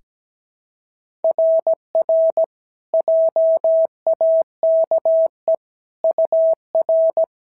Morse code, heard as a keyed tone, about 17 wpm, 660 Hz. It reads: RR JAKE UR
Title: SOTA QSO Practice 17wpm